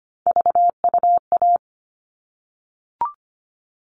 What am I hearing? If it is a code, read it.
Morse code, heard as a keyed tone, about 25 wpm, 700 Hz. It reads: VUA
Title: All Cut Numbers - 3-Digit 25wpm